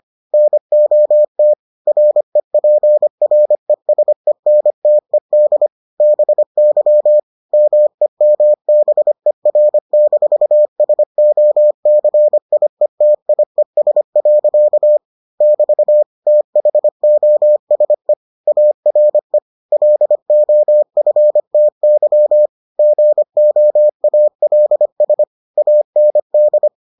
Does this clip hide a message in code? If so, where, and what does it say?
Yes: Morse code from 0.3 to 26.7 s, heard as a keyed tone, about 25 wpm, 600 Hz. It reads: NOT REPRESENTED BY MEMBER-SOCIETIES. <BT> THOSE ARE LOFTY GOALS AND